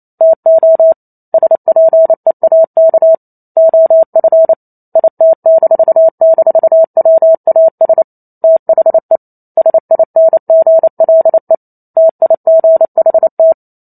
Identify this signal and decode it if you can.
Morse code, heard as a keyed tone, about 28 wpm, 650 Hz. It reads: TO SPEAK OF IT--WAS THE SINGLE TIGHT